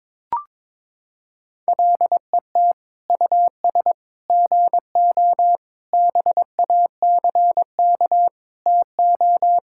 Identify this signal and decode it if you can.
Morse code, heard as a keyed tone, about 22 wpm, 700 Hz. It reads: LET US GO BACK TO
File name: practice / 22wpm / Sentences with 4-Word Phrases - 22wpm